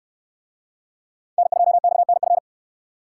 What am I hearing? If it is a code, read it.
Morse code, heard as a keyed tone, about 69 wpm, 700 Hz. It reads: N2XNF